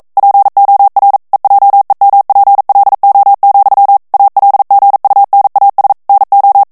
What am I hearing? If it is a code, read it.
Morse code, heard as a keyed tone, about 42 wpm, 800 Hz. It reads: POR EJEMPLO, ALGUNAS NO
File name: morse mp3 / ESPAGNOL TEXTE EN CLAIR 1 VITESSE 40 MOTS MINUTES